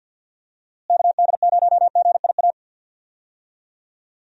Morse code, heard as a keyed tone, about 50 wpm, 700 Hz. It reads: KD0GIA
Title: USA Call Signs - 6 Characters - Repeated 3x - 50wpm